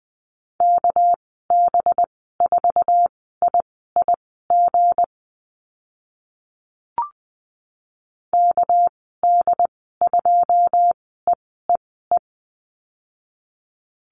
Morse code, heard as a keyed tone, about 20 wpm, spaced as for 14 wpm, 700 Hz. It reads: KB4IIG KD2EEE
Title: 219 Lesson 35g - Number 0 - Call Signs 20wpm